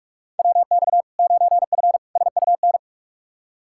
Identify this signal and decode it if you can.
Morse code, heard as a keyed tone, about 41 wpm, 700 Hz. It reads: WX 9F SUN
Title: Practice QSO - Exchange 2 - Info - 45wpm